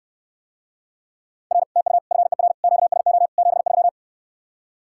Morse code, heard as a keyed tone, about 69 wpm, 700 Hz. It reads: R TU FER QSO 73